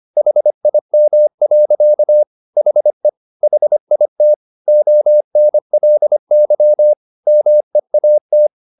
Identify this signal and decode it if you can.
Morse code, heard as a keyed tone, about 25 wpm, 600 Hz. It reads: HIM. HE HIT ONLY MEAT